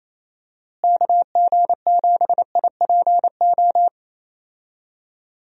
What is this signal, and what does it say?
Morse code, heard as a keyed tone, about 28 wpm, 700 Hz. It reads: KG7IPO
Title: USA Call Signs - 6 Characters - Repeated 2x - 28wpm